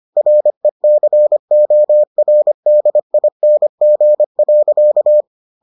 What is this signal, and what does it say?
Morse code, heard as a keyed tone, about 25 wpm, 600 Hz. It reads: RECORDING.